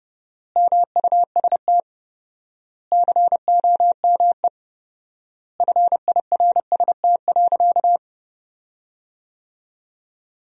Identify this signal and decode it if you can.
Morse code, heard as a keyed tone, about 32 wpm, 700 Hz. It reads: MUST COME FIRST.
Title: 4x Word-Spacing - Sentences from Top 300 Words - 30wpm